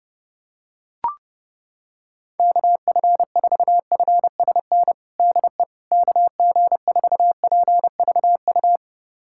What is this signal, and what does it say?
Morse code, heard as a keyed tone, about 32 wpm, 700 Hz. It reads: KF4FSN DE KG4PVU
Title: USA Callsign DE Callsign - 6 Characters - 30wpm